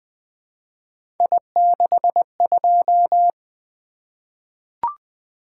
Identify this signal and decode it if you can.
Morse code, heard as a keyed tone, about 20 wpm, 700 Hz. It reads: I62
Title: Three Letter-Number Combinations - 20wpm